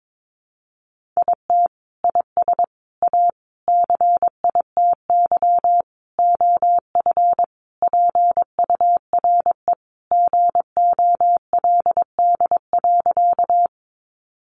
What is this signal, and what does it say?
Morse code, heard as a keyed tone, about 22 wpm, 700 Hz. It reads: IT IS A CITY OF PURE GOLD.
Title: Sentences from Top 1500 Words 22wpm